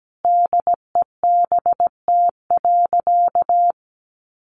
Morse code, heard as a keyed tone, about 17 wpm, 700 Hz. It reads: DEBT.
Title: Sentences from Top 1300 Words 17wpm